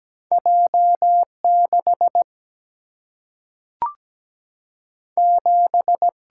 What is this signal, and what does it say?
Morse code, heard as a keyed tone, about 17 wpm, 700 Hz. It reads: J6 7
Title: Single-Letter Single-Number 17wpm